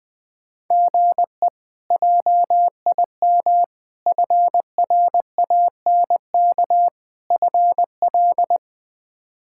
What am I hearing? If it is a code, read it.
Morse code, heard as a keyed tone, about 20 wpm, 700 Hz. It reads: GE JIM FRANK FL
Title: K1USN SST QSO Practice - 20wpm